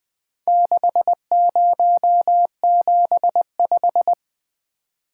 Morse code, heard as a keyed tone, about 20 wpm, 700 Hz. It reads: 6075
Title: Four Letter-Number Combinations - 20wpm